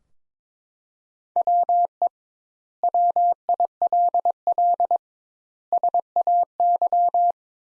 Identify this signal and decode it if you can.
Morse code, heard as a keyed tone, about 22 wpm, 700 Hz. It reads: WE WILL SAY
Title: Sentences Top 200 Words - 22wpm